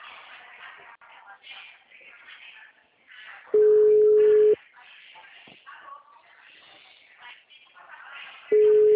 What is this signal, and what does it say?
Ringback tone (Europe, 425 Hz).
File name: out- -2038- - - .1732